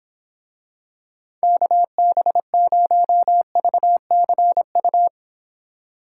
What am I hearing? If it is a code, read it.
Morse code, heard as a keyed tone, about 26 wpm, 700 Hz. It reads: KB0VCU